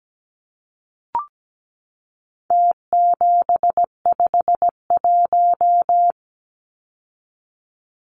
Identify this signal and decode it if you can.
Morse code, heard as a keyed tone, about 17 wpm, 700 Hz. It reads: T751
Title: Numbers - Contest Style - 4-Digit 17wpm